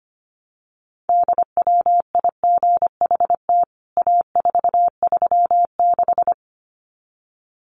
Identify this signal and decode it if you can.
Morse code, heard as a keyed tone, about 25 wpm, 700 Hz. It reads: DWIGHT A436
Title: CWT Exchange 25wpm